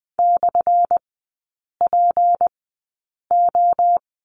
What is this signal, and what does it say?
Morse code, heard as a keyed tone, about 20 wpm, spaced as for 8 wpm, 700 Hz. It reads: /PO
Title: 205 Lesson 33e - Letter j - Three Characters 20wpm